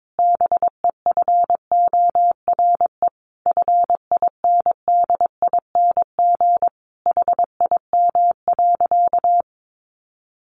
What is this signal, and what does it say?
Morse code, heard as a keyed tone, about 22 wpm, 700 Hz. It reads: BEFORE FINDING HIM.
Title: Sentences from Top 1100 Words 22wpm